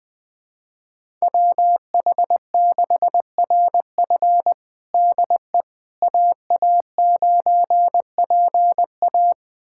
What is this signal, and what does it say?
Morse code, heard as a keyed tone, about 20 wpm, 700 Hz. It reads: WH6RF DE AA9PA
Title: USA Callsign DE Callsign - 5 Characters - 20wpm